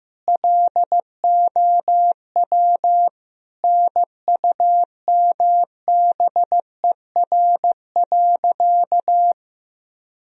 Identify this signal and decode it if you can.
Morse code, heard as a keyed tone, about 15 wpm, 700 Hz. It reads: LOW NUMBER.
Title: Sentences from Top 1000 Words - Review Part 1 - 15wpm